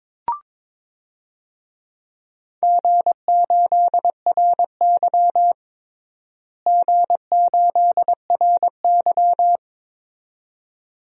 Morse code, heard as a keyed tone, about 22 wpm, 700 Hz. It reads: G8RY G8RY
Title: DXCC-100 Call Signs - 4 Characters - Repeated 2x - 22wpm